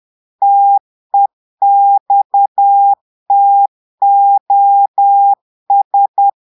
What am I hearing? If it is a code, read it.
Morse code, heard as a keyed tone, about 10 wpm, 800 Hz. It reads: TEXTOS
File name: Portugais - texte en clair 1 - vitesse - 10 mots minute